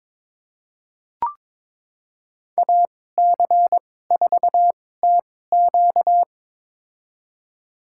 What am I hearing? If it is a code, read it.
Morse code, heard as a keyed tone, about 21 wpm, spaced as for 16 wpm, 700 Hz. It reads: AC4TQ
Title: USA Call Signs - 5 Characters - 2x Character Spacing - 22wpm